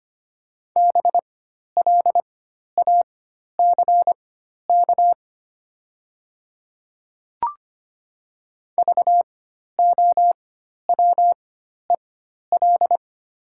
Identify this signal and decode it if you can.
Morse code, heard as a keyed tone, about 25 wpm, spaced as for 12 wpm, 700 Hz. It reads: BLACK VOWEL
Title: 4x Character Spacing - Top 500 Words - 25wpm